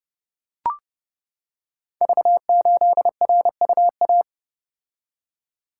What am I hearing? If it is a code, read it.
Morse code, heard as a keyed tone, about 29 wpm, 700 Hz. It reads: V8RUA
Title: Code Groups - 5 Letter:Numbers - 30wpm